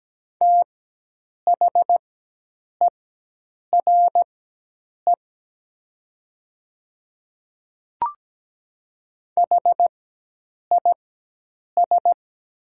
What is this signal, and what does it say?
Morse code, heard as a keyed tone, about 17 wpm, spaced as for 8 wpm, 700 Hz. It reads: THERE HIS
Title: 4x Character Spacing - Top 100 Words - 17wpm